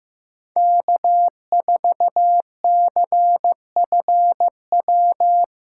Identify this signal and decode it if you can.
Morse code, heard as a keyed tone, about 15 wpm, 700 Hz. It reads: K4CFW
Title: USA Callsign DE Callsign 15wpm